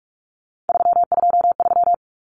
Morse code, heard as a keyed tone, about 44 wpm, 700 Hz. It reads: 323
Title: Common Cut Numbers - 3-Digit 45wpm